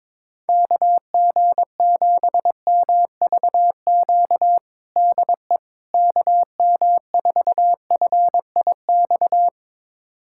Morse code, heard as a keyed tone, about 22 wpm, 700 Hz. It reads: KG7MVQ DE KM4FIX
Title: USA Callsign DE Callsign 22wpm